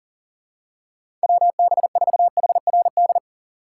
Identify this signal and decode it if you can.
Morse code, heard as a keyed tone, about 37 wpm, 700 Hz. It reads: WB4HRD
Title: USA Call Signs - 6 Characters - 3x Character Spacing - 40wpm